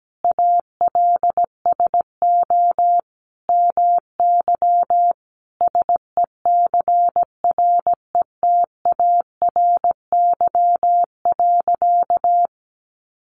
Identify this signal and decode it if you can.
Morse code, heard as a keyed tone, about 17 wpm, 700 Hz. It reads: ALSO MY SECRETARY.
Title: Sentences from Top 1300 Words 17wpm